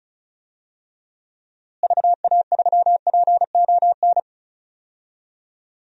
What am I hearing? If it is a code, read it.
Morse code, heard as a keyed tone, about 36 wpm, 700 Hz. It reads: VA3PON
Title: DXCC-100 Call Signs - 6 Characters - Repeated 3x - 35wpm